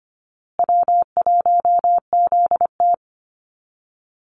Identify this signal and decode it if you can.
Morse code, heard as a keyed tone, about 25 wpm, 700 Hz. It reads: W1ZT
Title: Call Signs Repeated 2x - 25wpm